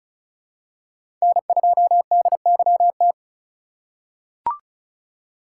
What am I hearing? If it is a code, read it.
Morse code, heard as a keyed tone, about 34 wpm, 700 Hz. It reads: N2DYT
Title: K1USN SST QSO Practice - 35wpm